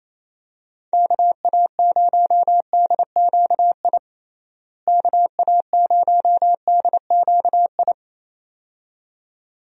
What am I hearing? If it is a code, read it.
Morse code, heard as a keyed tone, about 28 wpm, 700 Hz. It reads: KA0DQI KA0DQI
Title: USA Call Signs - 6 Characters - Repeated 2x - 28wpm